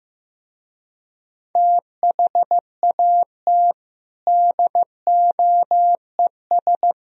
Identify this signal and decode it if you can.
Morse code, heard as a keyed tone, about 16 wpm, 700 Hz. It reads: THAT DOES
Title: Speed-Racing - Sets of 3 Words 15wpm